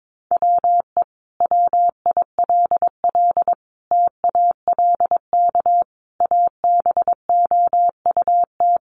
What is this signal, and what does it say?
Morse code, heard as a keyed tone, about 22 wpm, 700 Hz. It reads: WE WILL TALK ABOUT
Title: Sentences from Top 400 Words 22wpm